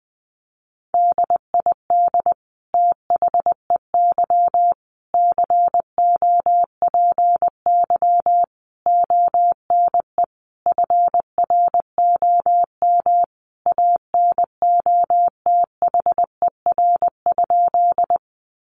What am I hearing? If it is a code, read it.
Morse code, heard as a keyed tone, about 20 wpm, 700 Hz. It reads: DID THEY COPY ONE FROM ANOTHER?
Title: Sentences from Top 1400 Words 20wpm